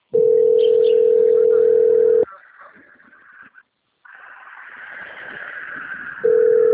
Ringback tone (North America), 440 + 480 Hz.